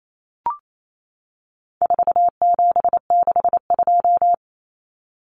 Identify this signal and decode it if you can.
Morse code, heard as a keyed tone, about 28 wpm, 700 Hz. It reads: 4762
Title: Numbers - 4-Digit 28wpm